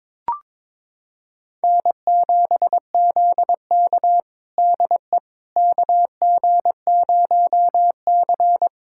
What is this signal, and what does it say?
Morse code, heard as a keyed tone, about 22 wpm, 700 Hz. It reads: N7ZK DE KG0C
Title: USA Callsign DE Callsign - 4 Characters - 22wpm